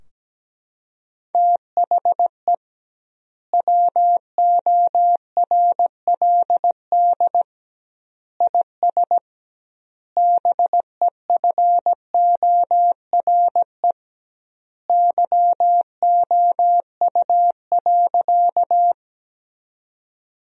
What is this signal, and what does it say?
Morse code, heard as a keyed tone, about 17 wpm, 700 Hz. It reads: THE WORLD IS BEFORE YOU.
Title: 2x Word-Spacing - Sentences from Top 200 Words - 17wpm